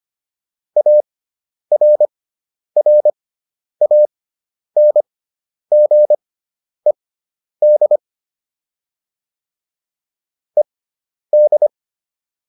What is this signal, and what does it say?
Morse code, heard as a keyed tone, about 25 wpm, spaced as for 10 wpm, 600 Hz. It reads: ARRANGED ED